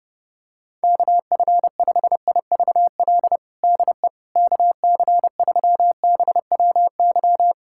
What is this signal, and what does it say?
Morse code, heard as a keyed tone, about 30 wpm, 700 Hz. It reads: KF5IVL DE KC3BWY
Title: USA Callsign DE Callsign - 6 Characters - 30wpm